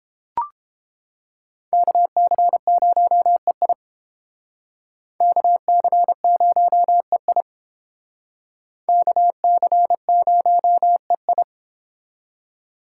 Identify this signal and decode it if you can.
Morse code, heard as a keyed tone, about 28 wpm, 700 Hz. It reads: KC0EI KC0EI KC0EI